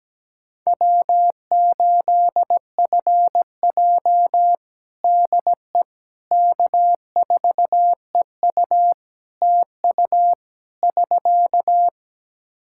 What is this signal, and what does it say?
Morse code, heard as a keyed tone, about 17 wpm, 700 Hz. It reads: W8FJ DE K4EU TU <SK>
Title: Practice QSO - Exchange 4 - Ending - 17wpm